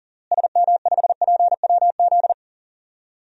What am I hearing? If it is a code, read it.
Morse code, heard as a keyed tone, about 40 wpm, 700 Hz. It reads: SM5PWZ
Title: DXCC-100 Call Signs - 6 Characters - Repeated 3x - 40wpm